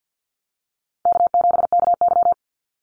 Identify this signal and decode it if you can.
Morse code, heard as a keyed tone, about 50 wpm, 700 Hz. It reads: K7KY
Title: Call Signs - 50wpm